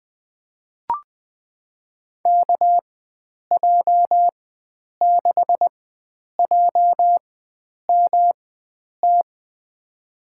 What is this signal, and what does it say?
Morse code, heard as a keyed tone, about 21 wpm, spaced as for 9 wpm, 700 Hz. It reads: KJ6JMT
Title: USA Call Signs - 6 Characters - 4x Character Spacing - 20wpm